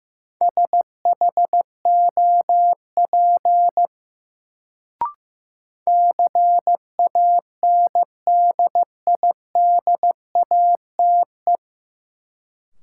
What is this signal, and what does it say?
Morse code, heard as a keyed tone, about 15 wpm, 700 Hz. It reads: SHOP CANDIDATE
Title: Sentences from Top 1600 Words - New Word Drill - 15wpm